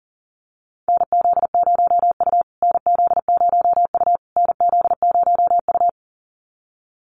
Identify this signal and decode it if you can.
Morse code, heard as a keyed tone, about 40 wpm, 700 Hz. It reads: NZ0U NZ0U NZ0U
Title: USA Call Signs Repeated 3x 40wpm